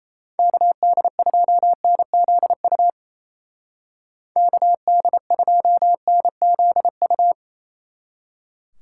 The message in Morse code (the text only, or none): KD2NZU KD2NZU